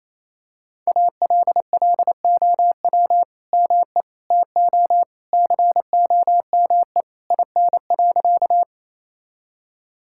Morse code, heard as a keyed tone, about 28 wpm, 700 Hz. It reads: ALLOW ME TO COME IN.
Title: Sentences from Top 1000 Words - Review Part 5 - 28wpm